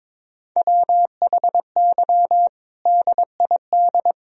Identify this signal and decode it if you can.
Morse code, heard as a keyed tone, about 22 wpm, 700 Hz. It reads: WHY DID
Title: Sentences from Top 1000 Words - Review Part 4 - 22wpm